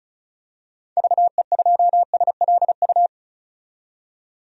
Morse code, heard as a keyed tone, about 36 wpm, 700 Hz. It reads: VE2SLU
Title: DXCC-100 Call Signs - 6 Characters - 35wpm